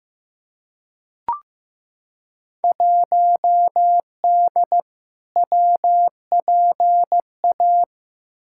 Morse code, heard as a keyed tone, about 15 wpm, 700 Hz. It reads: 1D WPA
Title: Field Day Exchanges - 15wpm